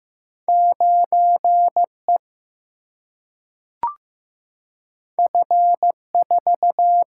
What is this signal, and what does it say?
Morse code, heard as a keyed tone, about 15 wpm, 700 Hz. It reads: 9E F4
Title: Single-Letter Single-Number 15wpm